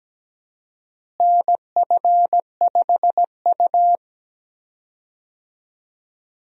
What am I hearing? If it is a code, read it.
Morse code, heard as a keyed tone, about 17 wpm, 700 Hz. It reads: NF5U
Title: USA Call Signs - 4 Characters - Repeated 2x - 17wpm